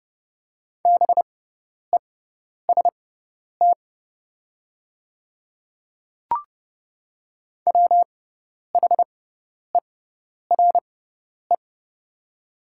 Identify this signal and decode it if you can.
Morse code, heard as a keyed tone, about 30 wpm, spaced as for 10 wpm, 700 Hz. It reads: BEST WHERE
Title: 6x Character Spacing - Top 500 Words - 30wpm